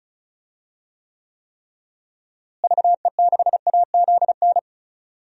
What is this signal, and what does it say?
Morse code, heard as a keyed tone, about 36 wpm, 700 Hz. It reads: VE6AZN
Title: DXCC-100 Call Signs - 6 Characters - Repeated 3x - 35wpm